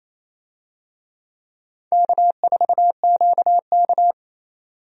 Morse code, heard as a keyed tone, about 28 wpm, 700 Hz. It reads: K4QK
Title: USA Call Signs - 4 Characters - 28wpm